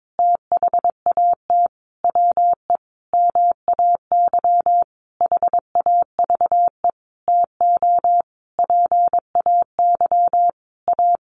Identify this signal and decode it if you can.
Morse code, heard as a keyed tone, about 22 wpm, 700 Hz. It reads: THAT WE MAY HAVE TO PAY A
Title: Sentences from Top 1100 Words 22wpm